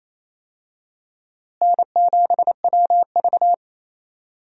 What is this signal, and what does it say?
Morse code, heard as a keyed tone, about 28 wpm, 700 Hz. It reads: N7WV
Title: USA Call Signs - 4 Characters - Repeated 2x - 28wpm